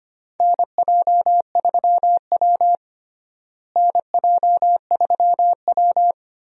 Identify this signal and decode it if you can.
Morse code, heard as a keyed tone, about 25 wpm, 700 Hz. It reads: NJ3W NJ3W
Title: USA Call Signs - 4 Characters - Repeated 2x - 25wpm